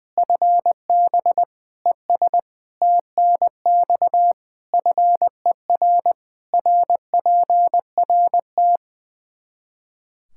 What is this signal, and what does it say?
Morse code, heard as a keyed tone, about 20 wpm, 700 Hz. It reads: FB ES TNX FER RPRT
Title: Practice QSO - Exchange 2 - Info - 20wpm